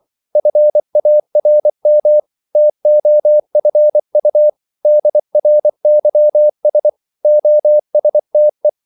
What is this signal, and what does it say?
Morse code, heard as a keyed tone, about 24 wpm, 600 Hz. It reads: FARM TOFU DRYS OSTE
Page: Korte danske ord 24wpm